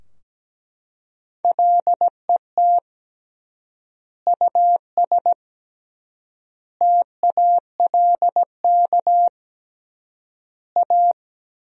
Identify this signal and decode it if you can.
Morse code, heard as a keyed tone, about 17 wpm, 700 Hz. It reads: LET US TALK A
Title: 3x Word-Spacing - Sentences from Top 400 Words - 17wpm